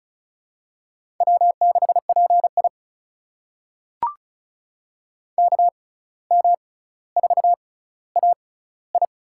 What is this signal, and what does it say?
Morse code, heard as a keyed tone, about 36 wpm, spaced as for 17 wpm, 700 Hz. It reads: W6PI KM4AI
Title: USA Call Signs - 6x Character Spacing - 35wpm